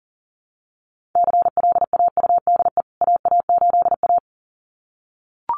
Morse code, heard as a keyed tone, about 40 wpm, 700 Hz. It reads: CLAUDE AA8A
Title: CWT Exchange 40wpm